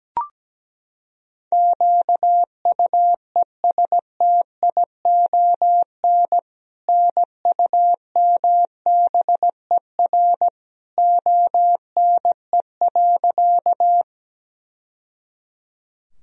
Morse code, heard as a keyed tone, about 17 wpm, 700 Hz. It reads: QUESTION NUMBER ONE.
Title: Sentences from Top 1000 Words - Review Part 1 - 17wpm